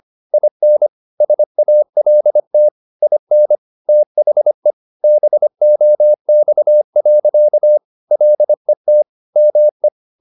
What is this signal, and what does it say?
Morse code, heard as a keyed tone, about 25 wpm, 600 Hz. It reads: IN SALT IN THE BOX. LET ME
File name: the old man and the sea part 1 25wpm